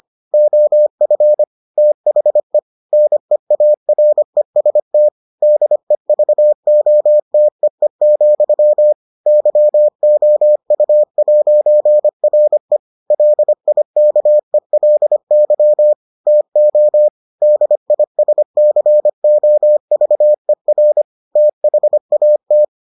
Morse code, heard as a keyed tone, about 25 wpm, 600 Hz. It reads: OF THE NEAREST DEVOTEE, YOU'RE LIKELY TO DISCOVER THAT